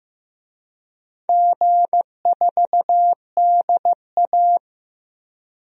Morse code, heard as a keyed tone, about 15 wpm, 700 Hz. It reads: G4DA
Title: DXCC-100 Call Signs - 4 Characters - 15wpm